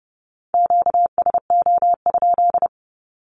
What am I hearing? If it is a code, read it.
Morse code, heard as a keyed tone, about 30 wpm, 700 Hz. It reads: QSO?
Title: Q-Codes 30wpm